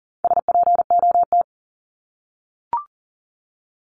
Morse code, heard as a keyed tone, about 40 wpm, 700 Hz. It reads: SPOT
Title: Top 1001-1250 Words 40wpm